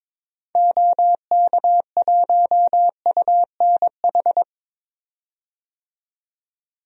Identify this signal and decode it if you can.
Morse code, heard as a keyed tone, about 21 wpm, 700 Hz. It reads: OK1UNH
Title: DXCC-100 Call Signs - 6 Characters - 22wpm